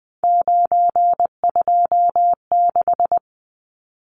Morse code, heard as a keyed tone, about 20 wpm, 700 Hz. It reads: 926
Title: Numbers - 3-Digit 20wpm